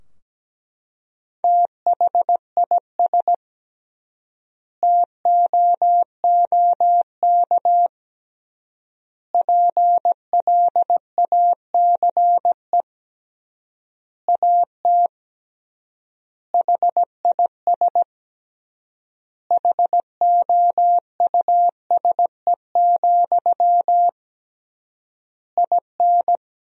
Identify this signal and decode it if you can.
Morse code, heard as a keyed tone, about 17 wpm, 700 Hz. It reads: THIS TOOK PLACE AT HIS HOUSE, IN
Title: 3x Word-Spacing - Sentences from Top 300 Words - 17wpm